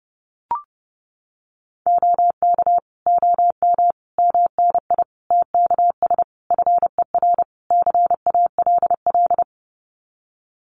Morse code, heard as a keyed tone, about 30 wpm, 700 Hz. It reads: OK OM MNI TKS FER CALL
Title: Common QSO Phrases 30wpm